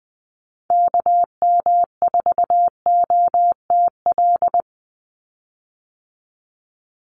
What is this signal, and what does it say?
Morse code, heard as a keyed tone, about 20 wpm, 700 Hz. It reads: KM4OTL
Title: USA Call Signs 20wpm